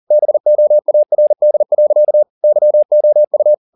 Morse code, heard as a keyed tone, about 40 wpm, 600 Hz. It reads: BOARD. YOU
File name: the old man and the sea 2 40wpm